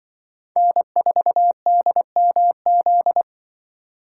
Morse code, heard as a keyed tone, about 24 wpm, 700 Hz. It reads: N4DMZ
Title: Speed Racing - USA 1x3 Call Signs 20wpm